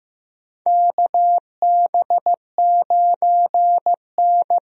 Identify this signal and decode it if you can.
Morse code, heard as a keyed tone, about 15 wpm, 700 Hz. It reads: KB9N
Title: USA Call Signs - 4 Characters - Repeated 2x - 15wpm